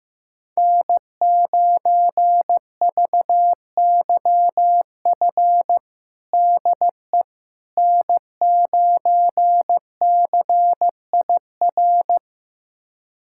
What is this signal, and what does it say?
Morse code, heard as a keyed tone, about 15 wpm, 700 Hz. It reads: N9VYF DE N9CIR
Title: USA Callsign DE Callsign - 5 Characters - 15wpm